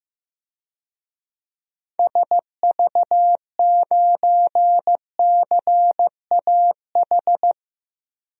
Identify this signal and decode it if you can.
Morse code, heard as a keyed tone, about 15 wpm, 700 Hz. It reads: SV9CAH
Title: DXCC-100 Call Signs - 6 Characters - 15wpm